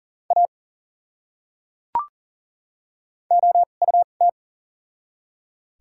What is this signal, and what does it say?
Morse code, heard as a keyed tone, about 40 wpm, spaced as for 8 wpm, 700 Hz. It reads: AOUT
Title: 2x Character Spacing - Top 200 Words - 40wpm